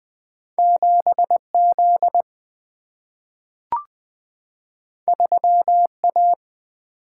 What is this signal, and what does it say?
Morse code, heard as a keyed tone, about 20 wpm, 700 Hz. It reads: 7Z 3A
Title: Single-Letter Single-Number 20wpm